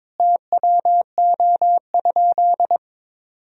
Morse code, heard as a keyed tone, about 22 wpm, 700 Hz. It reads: TWO?
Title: 3x Word-Spacing - Sentences from Top 100 Words - 22wpm